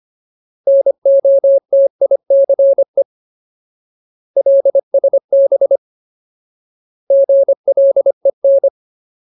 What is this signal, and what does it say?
Morse code, heard as a keyed tone, about 25 wpm, 550 Hz. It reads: NOTICE LSB GLEN
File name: Ultimate Word List / 25WPM Ultimate